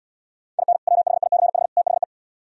Morse code, heard as a keyed tone, about 76 wpm, 700 Hz. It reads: IN WHICH THE